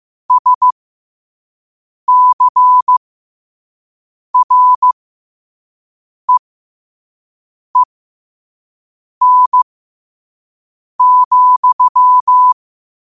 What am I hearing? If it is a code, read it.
Morse code, heard as a keyed tone, about 15 wpm, spaced as for 5 wpm, 1000 Hz. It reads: SCREEN,
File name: qotdfiles / 5wpm